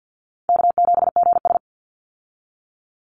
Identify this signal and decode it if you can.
Morse code, heard as a keyed tone, about 50 wpm, 700 Hz. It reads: K7GS
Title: Easy Call Signs Repeated 3x - 50wpm